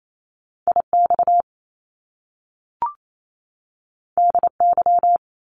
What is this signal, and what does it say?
Morse code, heard as a keyed tone, about 28 wpm, 700 Hz. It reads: IX DY
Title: Top Two Letter Combinations in Words 28wpm